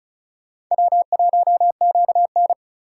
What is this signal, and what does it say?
Morse code, heard as a keyed tone, about 43 wpm, 700 Hz. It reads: W1QN
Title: Speed Racing - USA 1x2 Call Signs 35wpm